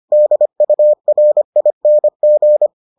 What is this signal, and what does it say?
Morse code, heard as a keyed tone, about 25 wpm, 600 Hz. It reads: DURING